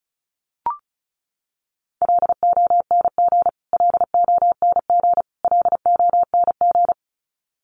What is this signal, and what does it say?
Morse code, heard as a keyed tone, about 34 wpm, 700 Hz. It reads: LONG LONG LONG
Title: Top 300 Words Repeated 35wpm